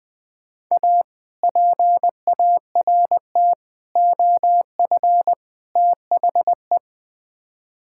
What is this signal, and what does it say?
Morse code, heard as a keyed tone, about 20 wpm, 690 Hz. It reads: A PART OF THE
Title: Build A Phrase - 4-Word Phrases - 20wpm